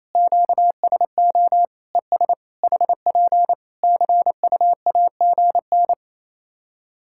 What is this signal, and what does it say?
Morse code, heard as a keyed tone, about 28 wpm, 700 Hz. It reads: QSO ES HP CUAGN
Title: Practice QSO - Exchange 4 - Ending - 28wpm